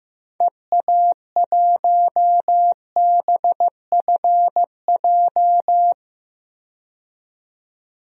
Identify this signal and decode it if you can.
Morse code, heard as a keyed tone, about 15 wpm, 700 Hz. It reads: EA1BFJ
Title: DXCC-100 Call Signs - 6 Characters - 15wpm